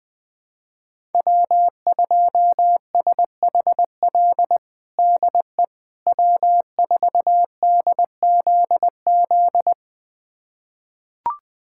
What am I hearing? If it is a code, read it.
Morse code, heard as a keyed tone, about 20 wpm, 700 Hz. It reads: W2SHL DE W4DZZ
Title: USA Callsign DE Callsign - 5 Characters - 20wpm